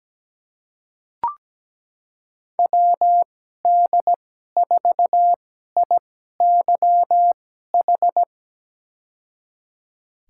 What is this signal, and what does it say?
Morse code, heard as a keyed tone, about 17 wpm, spaced as for 12 wpm, 700 Hz. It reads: WD4IYH
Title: USA Call Signs - 6 Characters - 2x Character Spacing - 17wpm